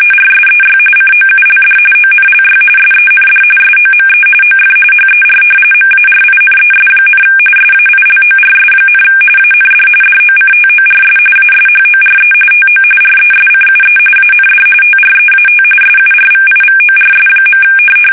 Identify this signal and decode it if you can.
RTTY, about 45.45 baud. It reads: RSEY COULD ONLY SEND SOME OF HER MARINES, SAILORS, AND DOCTORS ASHORE TO RESCUE AND TREAT SURVIVORS.
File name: fsk850 / battleshipnj